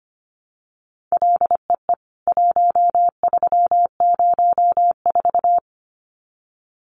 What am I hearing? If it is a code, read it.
Morse code, heard as a keyed tone, about 25 wpm, 700 Hz. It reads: LEE 1304
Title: CWT Exchange 25wpm